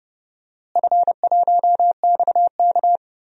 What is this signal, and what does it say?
Morse code, heard as a keyed tone, about 30 wpm, 700 Hz. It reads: F1XK
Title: DXCC-100 Call Signs - 4 Characters - Repeated 2x - 30wpm